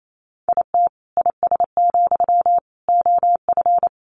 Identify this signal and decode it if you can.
Morse code, heard as a keyed tone, about 28 wpm, 700 Hz. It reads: IT IS, OF
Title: Sentences from Top 1200 Words 28wpm